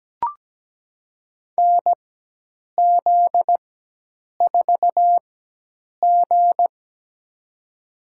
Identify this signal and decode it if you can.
Morse code, heard as a keyed tone, about 17 wpm, spaced as for 8 wpm, 700 Hz. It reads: NZ4G
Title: USA Call Signs - 4 Characters - 4x Character Spacing - 17wpm